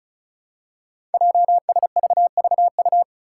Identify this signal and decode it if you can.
Morse code, heard as a keyed tone, about 36 wpm, 700 Hz. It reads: JSVVU
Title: Five Challenging Letter-Number Combinations 35wpm